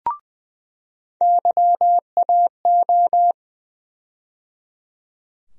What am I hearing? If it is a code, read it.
Morse code, heard as a keyed tone, about 20 wpm, 700 Hz. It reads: YAO
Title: USA Call Sign Postfixes - 3 Letters 20wpm